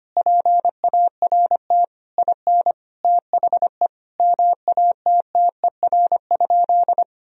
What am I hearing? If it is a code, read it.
Morse code, heard as a keyed tone, about 25 wpm, 700 Hz. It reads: PART IN THE MATTER?
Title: Sentences from Top 1000 Words - Review Part 1 - 25wpm